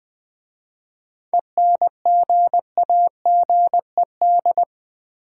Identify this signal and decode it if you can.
Morse code, heard as a keyed tone, about 20 wpm, 700 Hz. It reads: ENGAGED
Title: 7 Letter Words - First 3 Letters Given - 20wpm